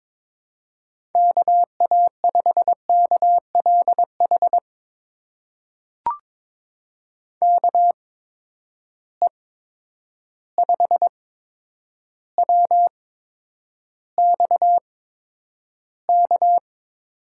Morse code, heard as a keyed tone, about 22 wpm, spaced as for 9 wpm, 700 Hz. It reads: KA5KLH KE5WXK
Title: USA Call Signs - 8x Character Spacing - 22wpm